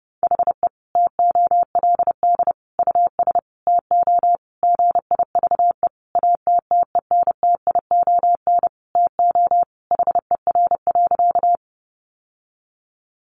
Morse code, heard as a keyed tone, about 30 wpm, 700 Hz. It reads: HE TOLD US TO GIVE ATTENTION TO HER.
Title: Sentences from Top 1000 Words 30wpm